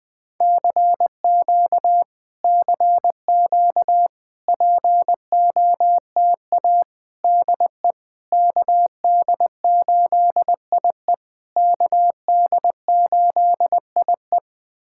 Morse code, heard as a keyed tone, about 20 wpm, 700 Hz. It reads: CQ CQ POTA DE KD8IE KD8IE
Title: POTA QSO Practice - 20wpm